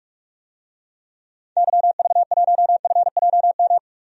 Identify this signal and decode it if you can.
Morse code, heard as a keyed tone, about 46 wpm, 700 Hz. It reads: YV1FJM
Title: DXCC-100 Call Signs - 6 Characters - 45wpm